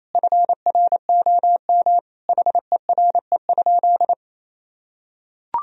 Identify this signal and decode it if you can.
Morse code, heard as a keyed tone, about 27 wpm, 700 Hz. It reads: FROM HERE?
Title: Sentences from Top 1000 Words - Review Part 4 - 28wpm